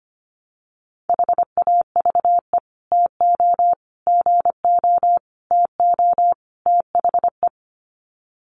Morse code, heard as a keyed tone, about 25 wpm, 700 Hz. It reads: HAVE TO GO TO THE
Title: Sets of 5 Words Encore 25wpm